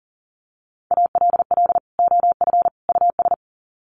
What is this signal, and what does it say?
Morse code, heard as a keyed tone, about 40 wpm, 700 Hz. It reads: ALL OF US
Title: Sets of 3 Words 40wpm